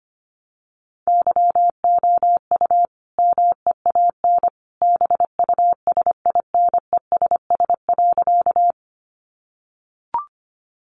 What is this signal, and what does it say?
Morse code, heard as a keyed tone, about 25 wpm, 700 Hz. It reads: YOU MEAN BUSINESS.
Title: Sentences from Top 600 Words 25wpm